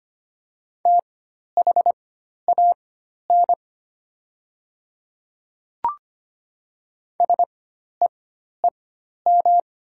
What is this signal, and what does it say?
Morse code, heard as a keyed tone, about 25 wpm, spaced as for 12 wpm, 700 Hz. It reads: THAN SEEM
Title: 4x Character Spacing - Top 500 Words - 25wpm